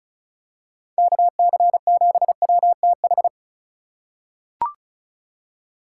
Morse code, heard as a keyed tone, about 36 wpm, 700 Hz. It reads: KC7WTH
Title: USA Call Signs - 6 Characters - 3x Character Spacing - 35wpm